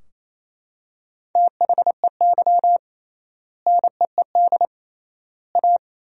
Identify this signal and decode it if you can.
Morse code, heard as a keyed tone, about 27 wpm, 700 Hz. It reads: THEY NEED A
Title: 3x Word-Spacing - Sentences from Top 500 Words - 28wpm